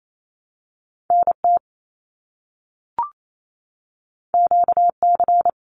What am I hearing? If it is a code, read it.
Morse code, heard as a keyed tone, about 28 wpm, 700 Hz. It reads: NT QC
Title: Canadian Provinces and Territories - Repeated 3x - 28wpm